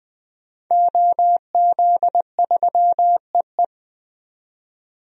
Morse code, heard as a keyed tone, about 20 wpm, 700 Hz. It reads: OZ3EE
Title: DXCC-100 Call Signs - 5 Characters - 20wpm